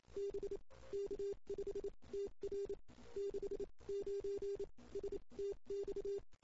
Morse code, heard as a keyed tone, about 28 wpm, spaced as for 23 wpm, 400 Hz. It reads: B K5TR 69 STX
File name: CW / 2008-ss-cw-k5tr-op-n5rz -03:00